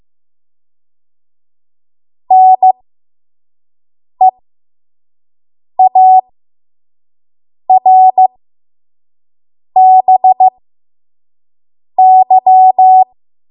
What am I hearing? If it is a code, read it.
Morse code, heard as a keyed tone, about 15 wpm, spaced as for 5 wpm, 750 Hz. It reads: NEARBY
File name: Morse / 05WPM